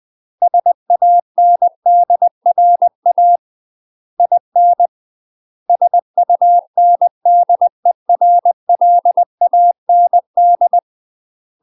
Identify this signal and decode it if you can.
Morse code, heard as a keyed tone, about 20 wpm, 700 Hz. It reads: SANDRA IN SUNDERLAND
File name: UK MASTER QSOs TRAINING / 20WPM / UK MASTER QSOs 0067